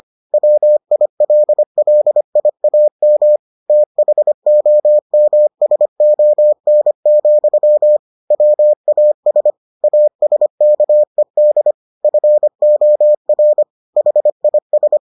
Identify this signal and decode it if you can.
Morse code, heard as a keyed tone, about 25 wpm, 600 Hz. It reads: WILLIAM THOMSON, WAS ASKED FOR HIS